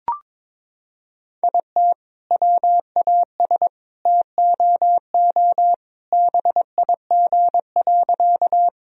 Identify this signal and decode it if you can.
Morse code, heard as a keyed tone, about 22 wpm, 700 Hz. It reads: IT WAS TOO BIG.
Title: Sentences from Top 1000 Words - Review Part 1 - 22wpm